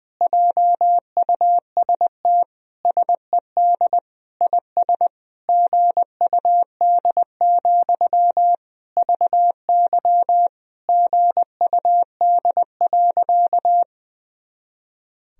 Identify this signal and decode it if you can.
Morse code, heard as a keyed tone, about 21 wpm, 700 Hz. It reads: JUST SED IS GUD, VY GUD.
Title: Sentences from Top 200 Words and Abbreviations 20wpm